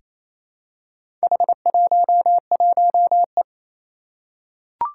Morse code, heard as a keyed tone, about 27 wpm, 700 Hz. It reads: H11E
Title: Four Challenging Letter-Number Combinations 28wpm